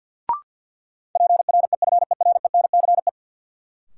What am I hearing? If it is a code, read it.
Morse code, heard as a keyed tone, about 49 wpm, 700 Hz. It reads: PREFERENCE
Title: Sentences from Top 1500 Words - New Word Drill Alt - 50wpm